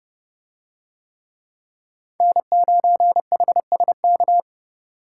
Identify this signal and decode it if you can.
Morse code, heard as a keyed tone, about 30 wpm, 700 Hz. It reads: N9HSK